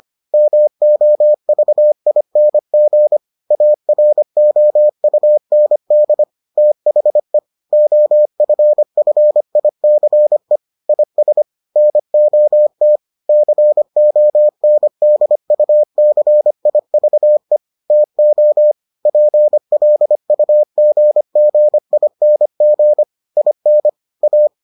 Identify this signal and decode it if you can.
Morse code, heard as a keyed tone, about 25 wpm, 600 Hz. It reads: MOVING AROUND THE OFFICE IS NOT CONDUCIVE TO PLUGGING IN A